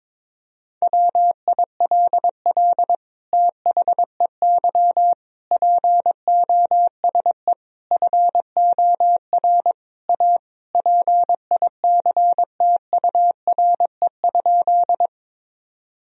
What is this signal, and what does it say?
Morse code, heard as a keyed tone, about 22 wpm, 700 Hz. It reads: WILL THEY POSE FOR A PICTURE?
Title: Sentences from Top 1000 Words - Review Part 2 - 22wpm